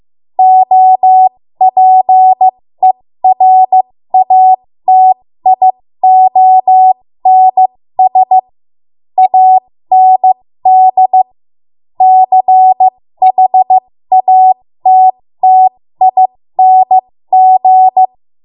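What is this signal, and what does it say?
Morse code, heard as a keyed tone, about 15 wpm, spaced as for 13 wpm, 750 Hz. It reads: OPERATIONS AND CHATTING
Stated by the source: ARRL Morse code practice at 13 WPM.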